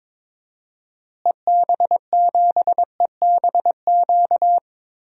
Morse code, heard as a keyed tone, about 22 wpm, 700 Hz. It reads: EB7EBQ
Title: DXCC-100 Call Signs - 6 Characters - Repeated 3x - 22wpm